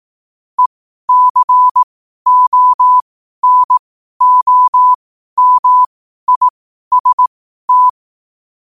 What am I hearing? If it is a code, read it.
Morse code, heard as a keyed tone, about 18 wpm, spaced as for 12 wpm, 1000 Hz. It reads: ECONOMIST
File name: qotdfiles / 12wpm